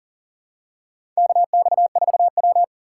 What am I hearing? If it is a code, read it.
Morse code, heard as a keyed tone, about 40 wpm, 700 Hz. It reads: KX4W
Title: USA Call Signs - 4 Characters - 6x Character Spacing - 40wpm